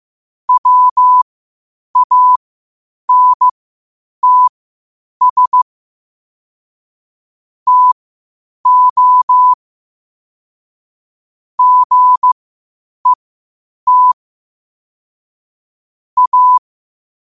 Morse code, heard as a keyed tone, about 15 wpm, spaced as for 8 wpm, 1000 Hz. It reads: WANTS TO GET A
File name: qotdfiles / 8wpm